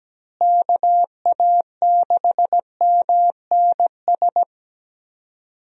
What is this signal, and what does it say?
Morse code, heard as a keyed tone, about 17 wpm, 700 Hz. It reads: KA6MNS
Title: USA Call Signs - 6 Characters - Repeated 3x - 17wpm